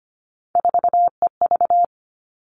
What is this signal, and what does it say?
Morse code, heard as a keyed tone, about 25 wpm, 700 Hz. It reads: 4EV
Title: Three Challenging Letter-Number Combinations 25wpm